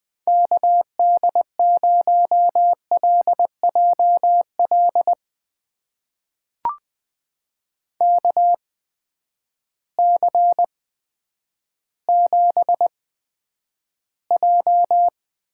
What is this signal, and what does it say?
Morse code, heard as a keyed tone, about 20 wpm, spaced as for 10 wpm, 700 Hz. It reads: KD0LJL KC7J
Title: USA Call Signs - 6 Characters - 8x Character Spacing - 20wpm